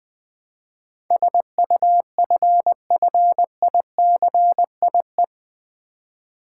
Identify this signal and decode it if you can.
Morse code, heard as a keyed tone, about 20 wpm, 700 Hz. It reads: SUFFICIE
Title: Build A Word - 4 Letters At A Time - 12-Letter Words - 20wpm